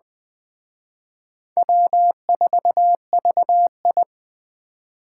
Morse code, heard as a keyed tone, about 20 wpm, 700 Hz. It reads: W4VI
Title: Speed Racing - USA 1x2 Call Signs 17wpm